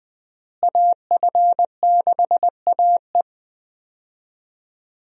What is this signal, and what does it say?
Morse code, heard as a keyed tone, about 20 wpm, 700 Hz. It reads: AF6AE
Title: USA Call Signs - 5 Characters - Repeated 3x - 20wpm